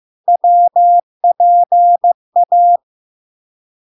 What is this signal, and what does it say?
Morse code, heard as a keyed tone, about 15 wpm, 700 Hz. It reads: WPA
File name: LARK-FD-2022-Log-in-CW-15wpm-700-Hz